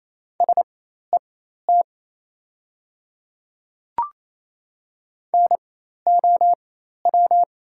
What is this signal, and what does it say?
Morse code, heard as a keyed tone, about 28 wpm, spaced as for 13 wpm, 700 Hz. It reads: SET NOW
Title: 4x Character Spacing - Top 200 Words - 28wpm